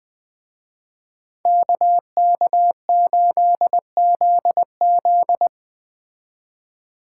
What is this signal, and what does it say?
Morse code, heard as a keyed tone, about 21 wpm, 700 Hz. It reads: KK8ZZ